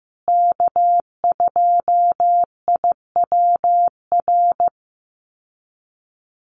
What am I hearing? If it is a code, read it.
Morse code, heard as a keyed tone, about 14 wpm, 700 Hz. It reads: K2IWR
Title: USA Call Signs Repeated 3x 15wpm